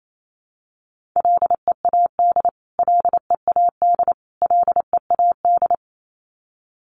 Morse code, heard as a keyed tone, about 28 wpm, 700 Hz. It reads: LEAD LEAD LEAD
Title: Top 500 Words Repeated 28wpm